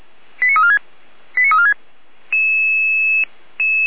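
Five-tone selective call (ZVEI), heard as quick runs of five tones, tones of about 70 ms. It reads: 89346 89346
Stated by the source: ZVEI five-tone selective call received with an SDR for Feuerwehr Orlamünde, a fire brigade.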